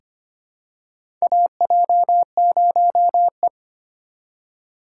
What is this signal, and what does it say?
Morse code, heard as a keyed tone, about 25 wpm, 700 Hz. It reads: AJ0E
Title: USA Call Signs - 4 Characters - 2x Character Spacing - 25wpm